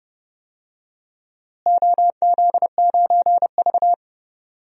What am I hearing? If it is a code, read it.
Morse code, heard as a keyed tone, about 30 wpm, 700 Hz. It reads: OZ9V
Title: DXCC-100 Call Signs - 4 Characters - Repeated 2x - 30wpm